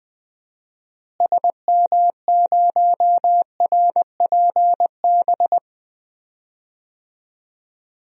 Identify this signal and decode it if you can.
Morse code, heard as a keyed tone, about 20 wpm, 700 Hz. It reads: SM0RPB
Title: DXCC-100 Call Signs - 6 Characters - Repeated 3x - 20wpm